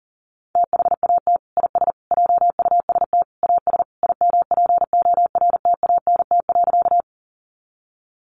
Morse code, heard as a keyed tone, about 40 wpm, 700 Hz. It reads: THAT IS JUST AS IMPORTANT.
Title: Sentences from Top 600 Words 40wpm